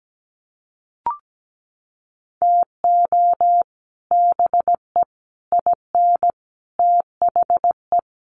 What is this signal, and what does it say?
Morse code, heard as a keyed tone, about 17 wpm, 700 Hz. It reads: TO BE IN THE
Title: Sets of 4 Words 17wpm